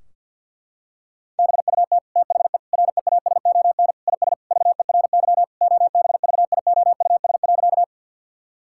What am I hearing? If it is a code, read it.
Morse code, heard as a keyed tone, about 50 wpm, 700 Hz. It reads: BUT THE REASON IS VERY OBVIOUS.
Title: Sentences from Top 1600 Words 50wpm